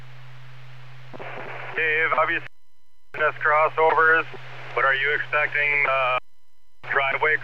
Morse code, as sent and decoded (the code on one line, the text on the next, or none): none